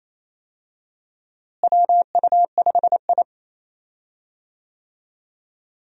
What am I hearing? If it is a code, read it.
Morse code, heard as a keyed tone, about 27 wpm, 700 Hz. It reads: WU5I
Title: USA Call Signs - 4 Characters - Repeated 2x - 28wpm